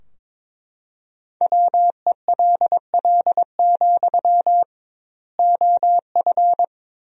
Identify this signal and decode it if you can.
Morse code, heard as a keyed tone, about 22 wpm, 700 Hz. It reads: WELL, OF
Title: 2x Word-Spacing - Sentences from Top 500 Words - 22wpm